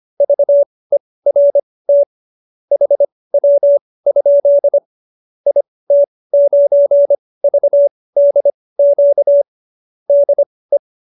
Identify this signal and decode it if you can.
Morse code, heard as a keyed tone, about 25 wpm, spaced as for 18 wpm, 570 Hz. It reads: VERT HW? IT9VDQ DE